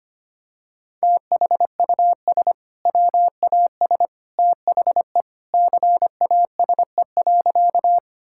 Morse code, heard as a keyed tone, about 25 wpm, 700 Hz. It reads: THUS WAS THE CASE.
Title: Sentences from Top 1000 Words - Review Part 5 - 25wpm